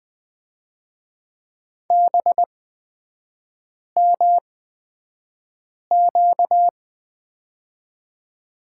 Morse code, heard as a keyed tone, about 20 wpm, spaced as for 5 wpm, 700 Hz. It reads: BMQ
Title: 236 Lesson 38e - Letter q - Three Characters 20wpm